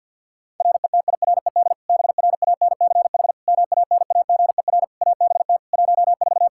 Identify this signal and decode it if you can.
Morse code, heard as a keyed tone, about 52 wpm, 700 Hz. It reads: RETIRED BRANCH MANAGER ABT 14
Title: Practice QSO - Exchange 3 - More Info - 50wpm